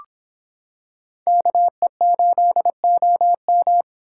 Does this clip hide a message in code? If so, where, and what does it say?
Yes: Morse code from 1.3 to 3.8 s, heard as a keyed tone, about 25 wpm, 700 Hz. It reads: KE8OM